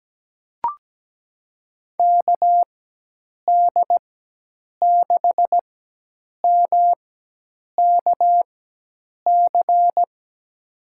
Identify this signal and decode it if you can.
Morse code, heard as a keyed tone, about 17 wpm, spaced as for 8 wpm, 700 Hz. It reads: KD6MKC
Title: USA Call Signs - 6 Characters - 4x Character Spacing - 17wpm